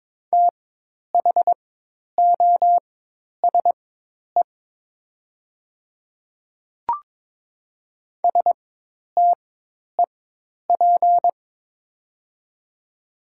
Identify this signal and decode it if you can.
Morse code, heard as a keyed tone, about 22 wpm, spaced as for 10 wpm, 700 Hz. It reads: THOSE STEP
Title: 4x Character Spacing - Top 500 Words - 22wpm